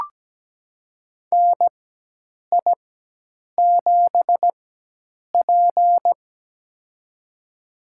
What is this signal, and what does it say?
Morse code, heard as a keyed tone, about 17 wpm, spaced as for 8 wpm, 700 Hz. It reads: NI7P
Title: USA Call Signs - 4 Characters - 4x Character Spacing - 17wpm